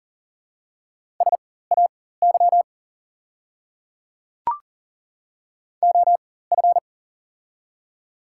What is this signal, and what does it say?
Morse code, heard as a keyed tone, about 40 wpm, spaced as for 19 wpm, 690 Hz. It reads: SAY OF
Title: 4x Character Spacing - Top 200 Words - 40wpm